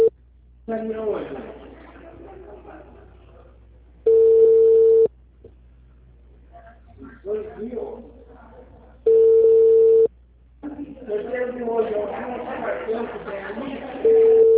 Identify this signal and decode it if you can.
Ringback tone (China, 450 Hz), with speech.